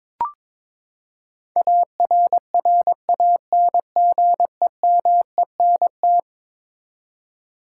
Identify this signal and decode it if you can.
Morse code, heard as a keyed tone, about 22 wpm, 700 Hz. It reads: ARRANGEMENT
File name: practice / 22wpm / Eleven Letter Words - 22wpm